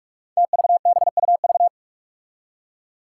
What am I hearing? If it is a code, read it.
Morse code, heard as a keyed tone, about 44 wpm, 700 Hz. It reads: TVBUV
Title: Five Challenging Letter-Number Combinations 45wpm